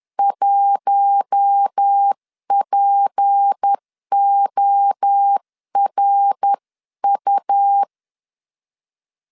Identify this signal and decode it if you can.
Morse code, heard as a keyed tone, about 10 wpm, 800 Hz. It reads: 1PORU
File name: CW / DET / 1 6 1500C S C 10wpm 4252